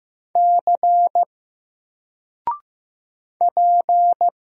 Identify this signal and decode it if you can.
Morse code, heard as a keyed tone, about 15 wpm, 700 Hz. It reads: C P
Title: Single Letter-Number 15wpm